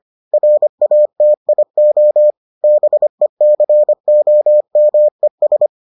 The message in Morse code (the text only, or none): RATIO BECOMES